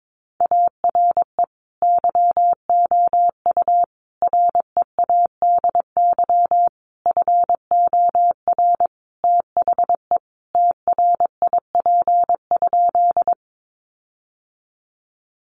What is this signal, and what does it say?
Morse code, heard as a keyed tone, about 22 wpm, 700 Hz. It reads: ARE YOU READY FOR THE TRIP?
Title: Sentences from Top 1300 Words 22wpm